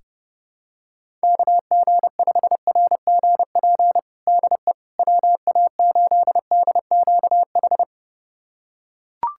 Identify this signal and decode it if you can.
Morse code, heard as a keyed tone, about 32 wpm, 700 Hz. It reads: KG5RGP DE WA8DQH
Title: USA Callsign DE Callsign 30wpm